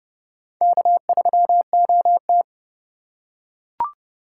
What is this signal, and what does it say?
Morse code, heard as a keyed tone, about 31 wpm, 700 Hz. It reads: K3OT
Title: USA Call Signs - 4 Characters - 6x Character Spacing - 30wpm